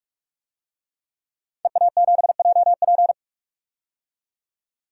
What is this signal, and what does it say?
Morse code, heard as a keyed tone, about 44 wpm, 700 Hz. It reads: EA7JP
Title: DXCC-100 Call Signs - 5 Characters - 45wpm